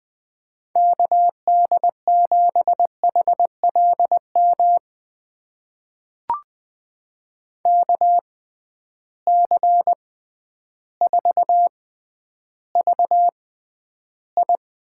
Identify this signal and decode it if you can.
Morse code, heard as a keyed tone, about 20 wpm, spaced as for 11 wpm, 700 Hz. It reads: KD7HLM KC4VI
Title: USA Call Signs - 6 Characters - 6x Character Spacing - 20wpm